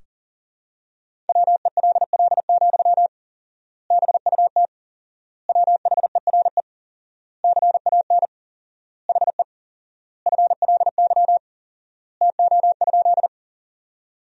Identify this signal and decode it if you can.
Morse code, heard as a keyed tone, about 40 wpm, 700 Hz. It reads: WELL, BUT WHERE CAN HE FLY TO?
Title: 4x Word-Spacing - Sentences from Top 500 Words - 40wpm